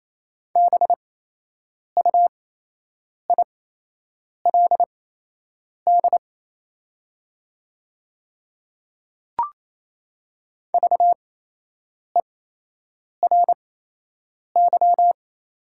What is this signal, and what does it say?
Morse code, heard as a keyed tone, about 28 wpm, spaced as for 8 wpm, 700 Hz. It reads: BUILD VERY
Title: 8x Character Spacing - Top 200 Words - 28wpm